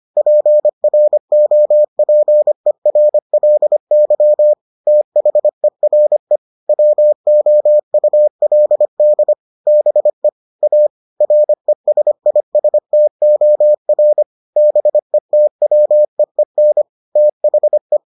Morse code, heard as a keyed tone, about 25 wpm, 600 Hz. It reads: PROPERLY THERE WOULD BE A RESISTOR BETWEEN THE